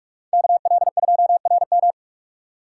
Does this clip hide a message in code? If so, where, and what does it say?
Yes: Morse code from 0.3 to 1.9 s, heard as a keyed tone, about 46 wpm, 700 Hz. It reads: KL2RM